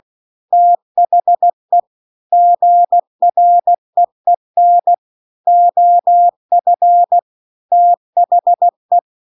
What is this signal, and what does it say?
Morse code, heard as a keyed tone, about 16 wpm, 700 Hz. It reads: THE GREEN OF THE